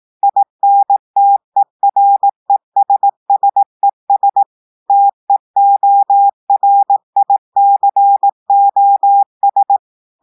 Morse code, heard as a keyed tone, about 18 wpm, 800 Hz. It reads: INTERESSES TEORICOS
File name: audios / portugais / portugais texte en clair 1 vitesse 18 mots minutes